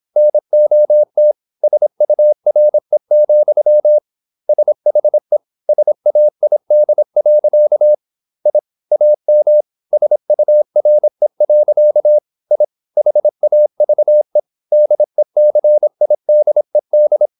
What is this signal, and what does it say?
Morse code, heard as a keyed tone, about 26 wpm, 600 Hz. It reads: NOT SURE, SHE SAID. I AM SURE. I HAVE DECIDED
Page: Never 26wpm | CW med Gnister